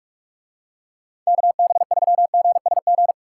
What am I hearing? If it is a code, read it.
Morse code, heard as a keyed tone, about 46 wpm, 700 Hz. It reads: KB3GSG